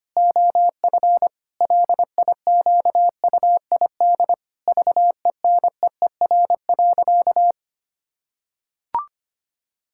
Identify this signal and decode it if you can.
Morse code, heard as a keyed tone, about 25 wpm, 700 Hz. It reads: OF LIQUID VENEER.
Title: Pangram Challenge v2 - 25wpm